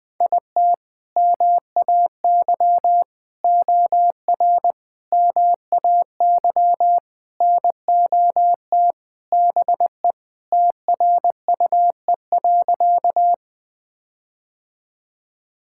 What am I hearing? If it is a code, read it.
Morse code, heard as a keyed tone, about 21 wpm, 700 Hz. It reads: IT MAY OR MAY NOT BE TRUE.
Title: Sentences with 5-Word Phrases - 20wpm